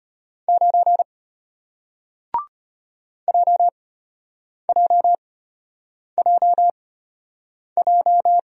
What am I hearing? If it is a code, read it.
Morse code, heard as a keyed tone, about 34 wpm, spaced as for 8 wpm, 700 Hz. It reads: 9 JJJJ